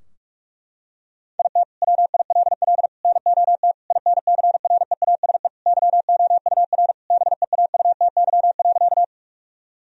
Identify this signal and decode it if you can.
Morse code, heard as a keyed tone, about 45 wpm, 700 Hz. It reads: IT WILL NOT INCREASE YOUR BEAUTY.
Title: Sentences from Top 1000 Words - Review Part 4 - 45wpm